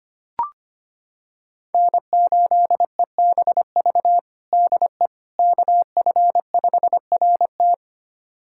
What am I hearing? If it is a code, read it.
Morse code, heard as a keyed tone, about 25 wpm, 700 Hz. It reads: N8EBV DE KF5RT
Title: USA Callsign DE Callsign - 5 Characters - 25wpm